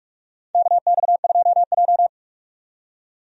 Morse code, heard as a keyed tone, about 46 wpm, 700 Hz. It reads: KX2J
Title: USA Call Signs - 4 Characters - 45wpm